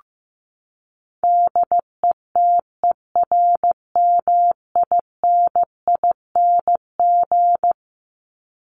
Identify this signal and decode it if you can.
Morse code, heard as a keyed tone, about 15 wpm, 700 Hz. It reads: DETERMINING
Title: ING Suffix - Top 100 Words with Base Words 15wpm